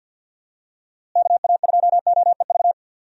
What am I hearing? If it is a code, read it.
Morse code, heard as a keyed tone, about 52 wpm, 700 Hz. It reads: KA2OEV
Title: USA Call Signs - 6 Characters - Repeated 2x - 50wpm